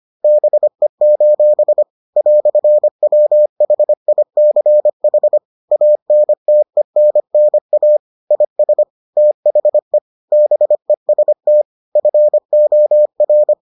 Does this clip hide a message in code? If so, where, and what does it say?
Yes: Morse code from 0.2 to 13.5 s, heard as a keyed tone, about 25 wpm, 600 Hz. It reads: BE: "WHICH ANTENNA IS THE BEST FOR